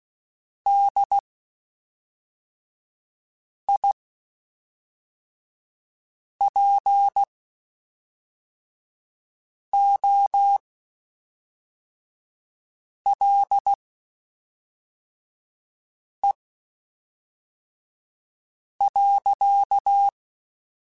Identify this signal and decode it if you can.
Morse code, heard as a keyed tone, about 16 wpm, spaced as for 3 wpm, 800 Hz. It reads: DIPOLE.
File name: (spaced) / QSOs Spaced-mp3s / WD0FBT 1 @5 sp